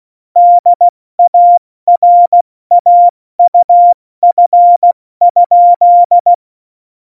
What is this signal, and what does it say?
Morse code, heard as a keyed tone, about 16 wpm, spaced as for 14 wpm, 700 Hz. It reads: DARAUF?
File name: Morsekurs / Witze / Witz 10. Oktober 17, Tempo 70